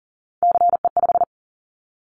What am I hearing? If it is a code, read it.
Morse code, heard as a keyed tone, about 40 wpm, 690 Hz. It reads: CE5
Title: Callsign Prefixes for Top 100 DXCC - 40wpm